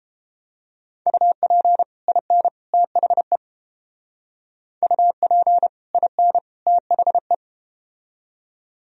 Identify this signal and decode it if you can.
Morse code, heard as a keyed tone, about 33 wpm, 700 Hz. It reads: UP IN THE UP IN THE